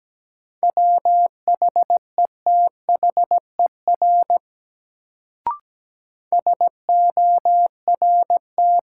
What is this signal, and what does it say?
Morse code, heard as a keyed tone, about 17 wpm, 700 Hz. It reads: WHETHER SORT
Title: Sentences from Top 1000 Words - Review Part 3 - New Word Drill - 17wpm